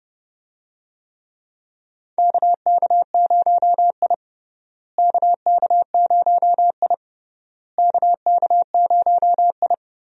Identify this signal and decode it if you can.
Morse code, heard as a keyed tone, about 30 wpm, 700 Hz. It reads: KK0I KK0I KK0I
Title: USA Call Signs - 4 Characters - Repeated 3x - 30wpm